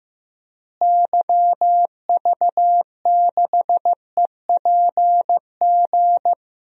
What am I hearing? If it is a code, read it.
Morse code, heard as a keyed tone, about 15 wpm, 700 Hz. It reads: YV6EPG
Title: DXCC-100 Call Signs - 6 Characters - Repeated 2x - 15wpm